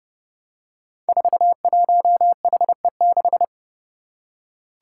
Morse code, heard as a keyed tone, about 30 wpm, 700 Hz. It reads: 41HE6
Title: Five Challenging Letter-Number Combinations 30wpm